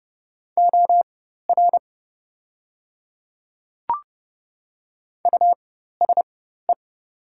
Morse code, heard as a keyed tone, about 30 wpm, spaced as for 14 wpm, 700 Hz. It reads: OR USE
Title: 4x Character Spacing - Top 200 Words - 30wpm